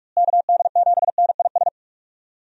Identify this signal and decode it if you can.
Morse code, heard as a keyed tone, about 41 wpm, 700 Hz. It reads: KD7NIS
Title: USA Call Signs - 6 Characters - Repeated 2x - 45wpm